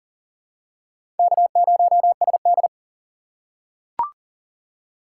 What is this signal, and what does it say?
Morse code, heard as a keyed tone, about 40 wpm, 700 Hz. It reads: K0SD
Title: USA Call Signs - 4 Characters - 6x Character Spacing - 40wpm